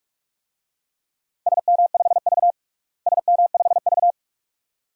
Morse code, heard as a keyed tone, about 46 wpm, 700 Hz. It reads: SM5V SM5V
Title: DXCC-100 Call Signs - 4 Characters - Repeated 2x - 45wpm